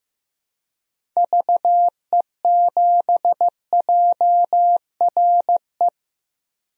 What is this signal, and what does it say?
Morse code, heard as a keyed tone, about 15 wpm, 700 Hz. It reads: VE7JRE
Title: DXCC-100 Call Signs - 6 Characters - 15wpm